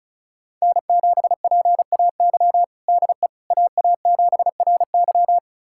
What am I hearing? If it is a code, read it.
Morse code, heard as a keyed tone, about 36 wpm, 700 Hz. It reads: N7PAY DE AA7RY
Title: USA Callsign DE Callsign - 5 Characters - 35wpm